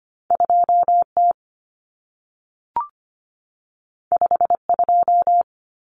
Morse code, heard as a keyed tone, about 25 wpm, 700 Hz. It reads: 2T 52
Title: Common Cut Numbers - 2-Digit 25wpm